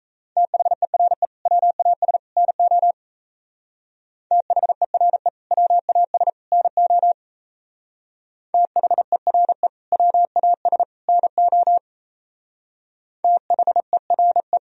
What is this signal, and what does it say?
Morse code, heard as a keyed tone, about 34 wpm, 700 Hz. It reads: THERE WAS NO THERE WAS NO THERE WAS NO THERE
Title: Speed-Racing - Sets of 3 Words 28wpm